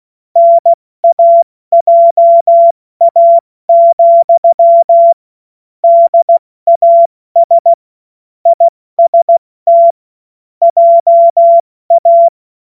Morse code, heard as a keyed tone, about 16 wpm, 680 Hz. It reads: NAJA, DAS IST JA
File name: Morsekurs / Witze / Witz 21. Dezember 2021, 70 F80